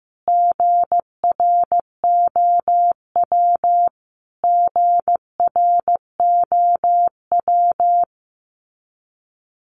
Morse code, heard as a keyed tone, about 15 wpm, 700 Hz. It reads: GROW GROW
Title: Top 500 Words Repeated 15wpm